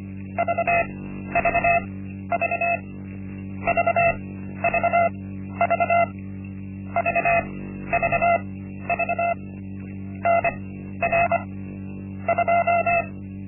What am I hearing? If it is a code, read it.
Morse code, heard as a keyed tone, about 24 wpm, spaced as for 12 wpm, 680 Hz. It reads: VVV VVV VVV NR 2
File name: Linksmakalnis-1